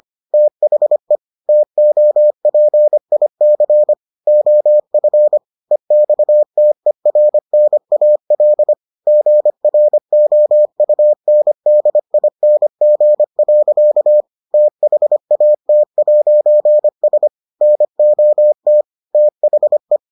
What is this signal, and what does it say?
Morse code, heard as a keyed tone, about 25 wpm, 600 Hz. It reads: THE TOPIC OF EXTERNAL GROUNDING. THAT'S NOT THE